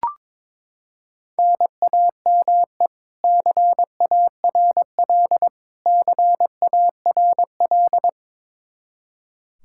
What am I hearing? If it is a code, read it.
Morse code, heard as a keyed tone, about 22 wpm, 700 Hz. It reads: NAME CARL CARL
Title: K1USN SST QSO Practice 22wpm